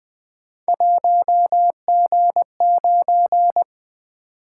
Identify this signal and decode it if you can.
Morse code, heard as a keyed tone, about 20 wpm, 700 Hz. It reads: 1G9
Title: Three Letter-Number Combinations - 20wpm